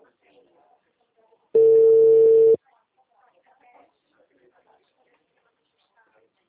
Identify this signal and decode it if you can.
Ringback tone (China, 450 Hz), with speech.
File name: out- -2009- - - .58466